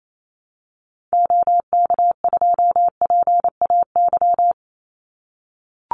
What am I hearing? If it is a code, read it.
Morse code, heard as a keyed tone, about 28 wpm, 700 Hz. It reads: OK2PAY
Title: Call Signs - 28wpm